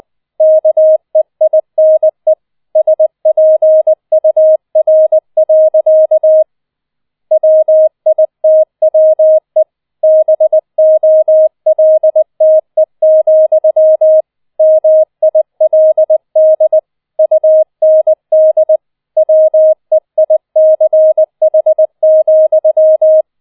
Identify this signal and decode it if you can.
Morse code, heard as a keyed tone, about 20 wpm, 630 Hz. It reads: KEINE SPUR. WITWE BOLTE, MILD UND WEICH,